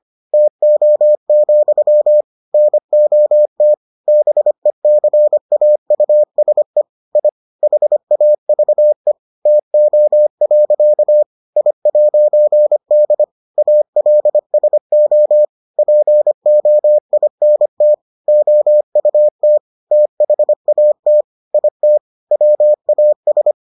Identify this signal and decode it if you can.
Morse code, heard as a keyed tone, about 25 wpm, 600 Hz. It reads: TO, NOT BECAUSE I HAVE TO. I'D ALSO POINT OUT THAT IT WAS